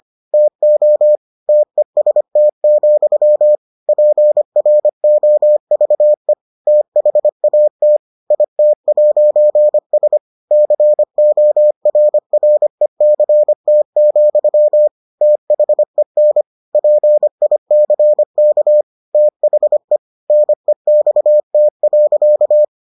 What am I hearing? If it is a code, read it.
Morse code, heard as a keyed tone, about 25 wpm, 600 Hz. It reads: TO TEST, PROVE THAT IT'S CORRECT, THEN PICK THE NEXT.